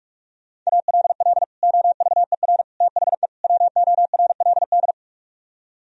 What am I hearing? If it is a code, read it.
Morse code, heard as a keyed tone, about 49 wpm, 700 Hz. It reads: ALL OVER THE WORLD